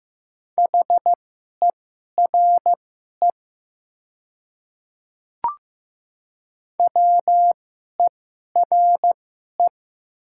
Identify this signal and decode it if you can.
Morse code, heard as a keyed tone, about 15 wpm, spaced as for 11 wpm, 700 Hz. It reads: HERE WERE
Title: 2x Character Spacing - Top 300 Words - 15wpm